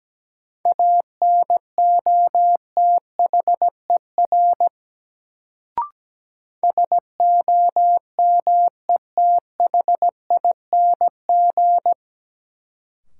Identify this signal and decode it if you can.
Morse code, heard as a keyed tone, about 17 wpm, 700 Hz. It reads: ANOTHER SOMETHING
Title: Sentences from Top 600 Words - New Word Drill - 17wpm